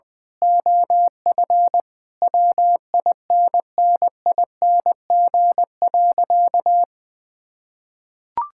Morse code, heard as a keyed tone, about 21 wpm, 700 Hz. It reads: OF WINNING.
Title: Sentences with 8-Word Phrases - 20wpm